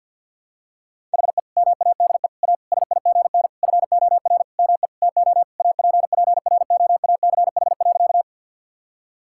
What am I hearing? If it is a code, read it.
Morse code, heard as a keyed tone, about 52 wpm, 700 Hz. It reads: HE MADE A SIGN FOR ME TO APPROACH.
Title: Sentences from Top 1000 Words - Review Part 6 - 50wpm